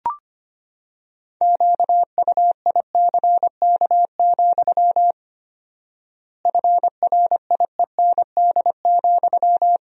Morse code, heard as a keyed tone, about 25 wpm, 700 Hz. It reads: QUICK, FRIEND,
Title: 4x Word-Spacing - Sentences from Top 500 Words - 25wpm